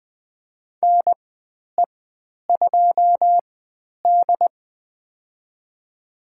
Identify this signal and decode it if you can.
Morse code, heard as a keyed tone, about 20 wpm, spaced as for 10 wpm, 700 Hz. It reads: NE2D
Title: 095 Lesson 16g - Letter c - Call Signs 20wpm